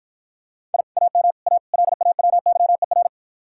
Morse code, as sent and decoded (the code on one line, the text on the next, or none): .. / .- -- / .- / .-.. .- .-- -.-- . .-.
I AM A LAWYER